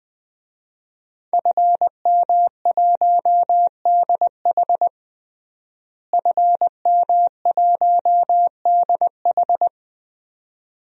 Morse code, heard as a keyed tone, about 20 wpm, 700 Hz. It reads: FM1DH FM1DH
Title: DXCC-100 Call Signs - 5 Characters - Repeated 2x - 20wpm